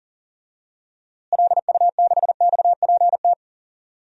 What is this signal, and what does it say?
Morse code, heard as a keyed tone, about 38 wpm, 700 Hz. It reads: LU6XPT
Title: DXCC-100 Call Signs - 6 Characters - Repeated 2x - 40wpm